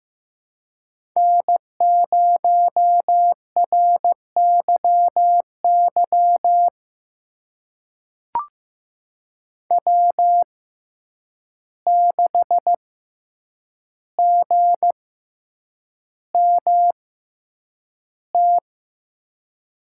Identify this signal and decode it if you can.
Morse code, heard as a keyed tone, about 15 wpm, spaced as for 8 wpm, 700 Hz. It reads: N0RYY W6GMT
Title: USA Call Signs - 5 Characters - 6x Character Spacing - 15wpm